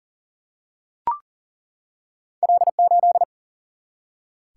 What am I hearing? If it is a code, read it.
Morse code, heard as a keyed tone, about 38 wpm, 700 Hz. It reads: L8
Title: Single-Letter Single-Number 40wpm